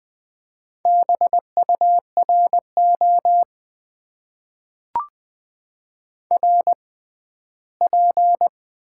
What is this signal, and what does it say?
Morse code, heard as a keyed tone, about 20 wpm, spaced as for 14 wpm, 700 Hz. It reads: BURO RP
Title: Top 90 QSO Elements - 6x Character Spacing - 20wpm